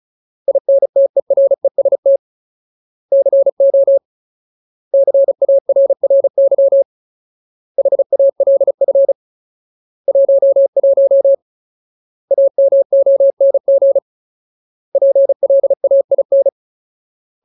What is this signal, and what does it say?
Morse code, heard as a keyed tone, about 35 wpm, 550 Hz. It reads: INTEREST CO CARRY HALF 11 AMONG PLAIN
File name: Ultimate Word List / 35WPM Ultimate